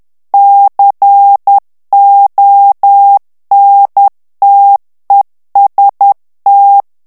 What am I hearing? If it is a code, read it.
Morse code, heard as a keyed tone, about 10 wpm, 800 Hz. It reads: CONTEST
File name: ITALIEN texte clair 2 vitesse 10 mots minute